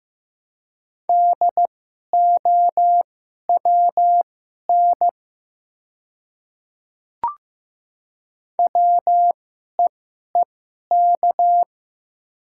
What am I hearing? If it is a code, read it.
Morse code, heard as a keyed tone, about 15 wpm, spaced as for 11 wpm, 700 Hz. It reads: DOWN WEEK
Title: 2x Character Spacing - Top 500 Words - 15wpm